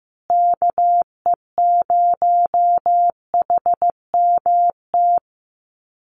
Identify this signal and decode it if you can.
Morse code, heard as a keyed tone, about 15 wpm, 700 Hz. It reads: KE0HMT
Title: USA Call Signs Repeated 2x 15wpm